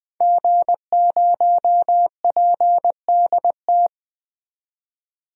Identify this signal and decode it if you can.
Morse code, heard as a keyed tone, about 21 wpm, 700 Hz. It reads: G0PDT
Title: DXCC-100 Call Signs - 5 Characters - 20wpm